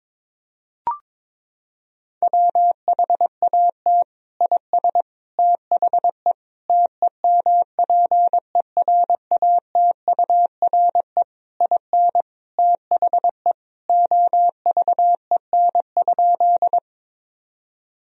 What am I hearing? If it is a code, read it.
Morse code, heard as a keyed tone, about 22 wpm, 700 Hz. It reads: WHAT IS THE TEMPERATURE IN THE OVEN?
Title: Sentences from Top 1000 Words - Review Part 4 - 22wpm